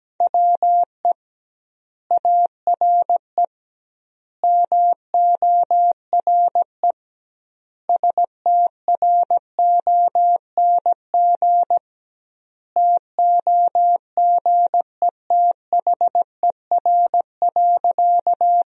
Morse code, heard as a keyed tone, about 17 wpm, 700 Hz. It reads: WE ARE MORE STRONG TOGETHER.
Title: 2x Word-Spacing - Sentences from Top 500 Words - 17wpm